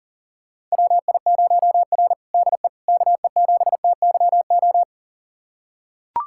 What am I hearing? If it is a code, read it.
Morse code, heard as a keyed tone, about 40 wpm, 700 Hz. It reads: WI0R DE KE7TYO
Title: USA Callsign DE Callsign 40wpm